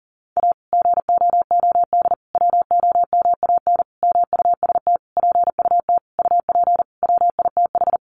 Morse code, heard as a keyed tone, about 40 wpm, 700 Hz. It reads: A GOOD WOMAN MUST PUT UP WITH
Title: Sentences from Top 1100 Words 40wpm